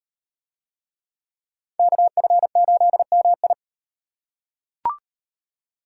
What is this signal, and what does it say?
Morse code, heard as a keyed tone, about 38 wpm, 700 Hz. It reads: KF8MI